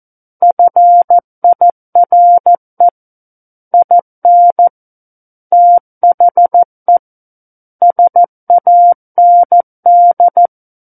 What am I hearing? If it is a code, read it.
Morse code, heard as a keyed tone, about 14 wpm, 700 Hz. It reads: FIRE IN THE SAND
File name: Book One / 05-Chapter 5 - 14 WPM